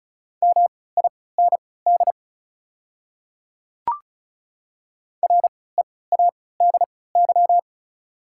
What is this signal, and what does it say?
Morse code, heard as a keyed tone, about 34 wpm, spaced as for 20 wpm, 690 Hz. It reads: MIND READY
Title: 3x Character Spacing - Top 500 Words - 35wpm